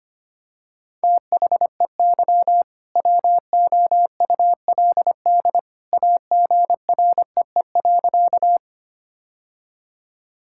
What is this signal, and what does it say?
Morse code, heard as a keyed tone, about 25 wpm, 700 Hz. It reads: THEY WOULD AGREE.
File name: practice / 25wpm / Sentences with 2-Word Phrases - 25wpm